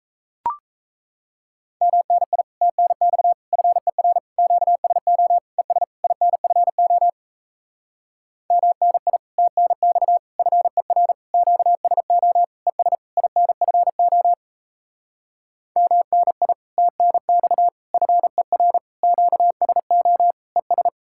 Morse code, heard as a keyed tone, about 38 wpm, 700 Hz. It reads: MNI TNX FER QSO ES INFO MNI TNX FER QSO ES INFO MNI TNX FER QSO ES